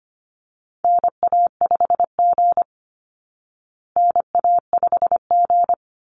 Morse code, heard as a keyed tone, about 25 wpm, 700 Hz. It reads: NA5G NA5G
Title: Call Signs Repeated 3x - 25wpm